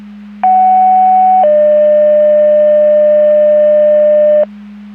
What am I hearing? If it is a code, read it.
Two-tone page, heard as a one-second tone, then a three-second tone, about 750 then 600 Hz, with noise. It is heard from 0.4 to 4.4 s.
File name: Portage 154.385 Paging